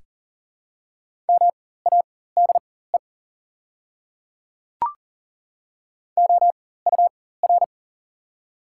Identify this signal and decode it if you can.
Morse code, heard as a keyed tone, about 40 wpm, spaced as for 19 wpm, 700 Hz. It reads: MADE OUR
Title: 4x Character Spacing - Top 500 Words - 40wpm